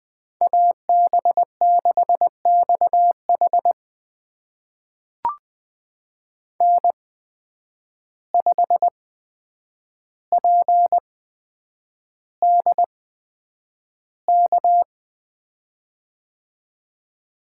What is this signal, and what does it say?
Morse code, heard as a keyed tone, about 20 wpm, spaced as for 9 wpm, 700 Hz. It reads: AB6XH N5PDK
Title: USA Call Signs - 5 Characters - 8x Character Spacing - 20wpm